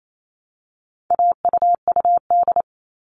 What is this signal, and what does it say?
Morse code, heard as a keyed tone, about 28 wpm, 700 Hz. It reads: AUUD
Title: All Cut Numbers - 4-Digit 28wpm